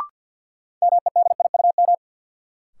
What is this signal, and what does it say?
Morse code, heard as a keyed tone, about 53 wpm, 700 Hz. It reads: MEDIUM
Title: Sentences from Top 900 Words - New Word Drill - 50wpm